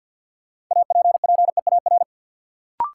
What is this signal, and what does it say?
Morse code, heard as a keyed tone, about 53 wpm, 700 Hz. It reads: APPEAR
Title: Sentences from Top 1200 Words - New Word Drill Alt - 50wpm